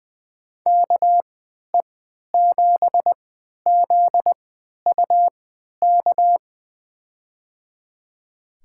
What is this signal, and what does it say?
Morse code, heard as a keyed tone, about 20 wpm, spaced as for 11 wpm, 700 Hz. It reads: KE7ZUK
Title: USA Call Signs - 6 Characters - 3x Character Spacing - 20wpm